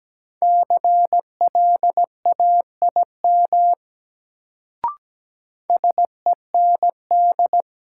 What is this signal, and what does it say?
Morse code, heard as a keyed tone, about 17 wpm, 700 Hz. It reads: CLAIM SEND
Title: Sentences from Top 1500 Words - New Word Drill - 17wpm